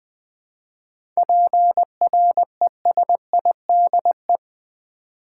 Morse code, heard as a keyed tone, about 20 wpm, 700 Hz. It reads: PRESIDE
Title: 7 Letter Words - First 3 Letters Given - 20wpm